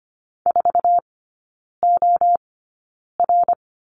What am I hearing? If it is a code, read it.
Morse code, heard as a keyed tone, about 25 wpm, spaced as for 9 wpm, 690 Hz. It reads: 4OR
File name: Lesson 15e - Number 5 - Three Characters 25wpm